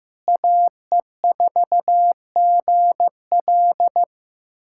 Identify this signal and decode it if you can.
Morse code, heard as a keyed tone, about 15 wpm, 700 Hz. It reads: AE4GL
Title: USA Callsign DE Callsign - 5 Characters - 15wpm